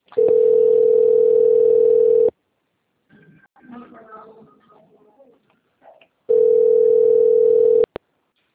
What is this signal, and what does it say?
Ringback tone (North America), 440 + 480 Hz, with speech.
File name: out- - - - - .73298